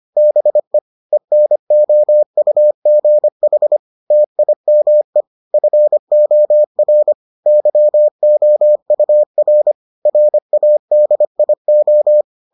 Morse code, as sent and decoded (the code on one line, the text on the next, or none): -... / . -. --- ..- --. .... / - .. -- . / ..-. --- .-. / -.-- --- ..- .-. / .-. .- -.. .. ---
BE ENOUGH TIME FOR YOUR RADIO